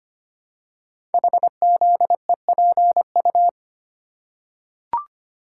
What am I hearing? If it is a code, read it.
Morse code, heard as a keyed tone, about 25 wpm, 700 Hz. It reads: HZEPU
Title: Code Groups - 5 Letters - 25wpm